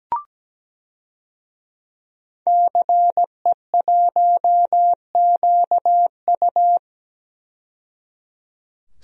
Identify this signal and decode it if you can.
Morse code, heard as a keyed tone, about 17 wpm, 700 Hz. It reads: CE1QU
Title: DXCC-100 Call Signs - 5 Characters - 17wpm